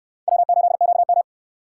Morse code, heard as a keyed tone, about 68 wpm, 700 Hz. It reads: W8QM